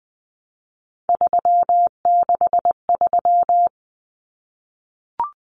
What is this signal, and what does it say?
Morse code, heard as a keyed tone, about 20 wpm, 700 Hz. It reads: 363
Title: Common Cut Numbers - 3-Digit 20wpm